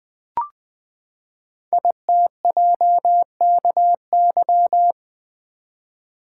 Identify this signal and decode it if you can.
Morse code, heard as a keyed tone, about 20 wpm, 700 Hz. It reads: ITJKY
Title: Code Groups - 5 Letters - 20wpm